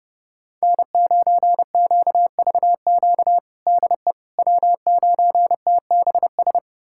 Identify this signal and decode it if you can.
Morse code, heard as a keyed tone, about 31 wpm, 700 Hz. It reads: N9QVQ DE W9TBS
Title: USA Callsign DE Callsign - 5 Characters - 30wpm